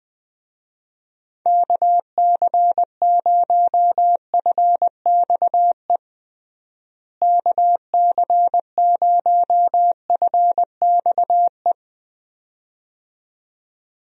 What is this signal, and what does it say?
Morse code, heard as a keyed tone, about 20 wpm, 700 Hz. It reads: KC0FXE KC0FXE
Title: USA Call Signs - 6 Characters - Repeated 2x - 20wpm